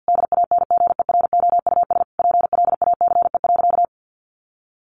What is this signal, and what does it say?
Morse code, heard as a keyed tone, about 50 wpm, 700 Hz. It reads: DANGEROUS PLACE.
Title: Sentences from Top 1200 Words 50wpm